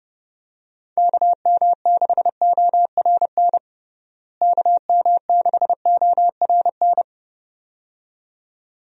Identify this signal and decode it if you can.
Morse code, heard as a keyed tone, about 30 wpm, 700 Hz. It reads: KM6ORN KM6ORN
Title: USA Call Signs - 6 Characters - Repeated 2x - 30wpm